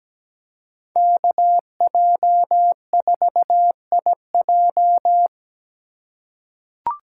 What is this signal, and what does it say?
Morse code, heard as a keyed tone, about 17 wpm, 700 Hz. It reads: KJ4IJ
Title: USA Call Signs - 5 Characters - 4x Character Spacing - 17wpm